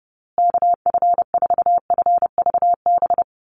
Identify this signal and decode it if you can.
Morse code, heard as a keyed tone, about 30 wpm, 700 Hz. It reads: KF4FVB
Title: USA Call Signs Repeated 3x 30wpm